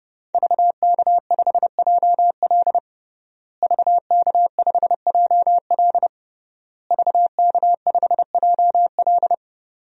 Morse code, heard as a keyed tone, about 32 wpm, 700 Hz. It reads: VK5JL VK5JL VK5JL
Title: DXCC-100 Call Signs - 5 Characters - Repeated 3x - 30wpm